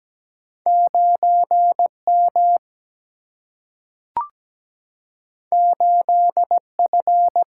Morse code, heard as a keyed tone, about 17 wpm, 700 Hz. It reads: 9 M 8 F
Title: Single-Letter Single-Number 17wpm